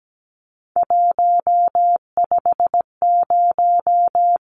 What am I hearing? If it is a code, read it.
Morse code, heard as a keyed tone, about 17 wpm, 700 Hz. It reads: 150
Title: Numbers - 3-Digit 17wpm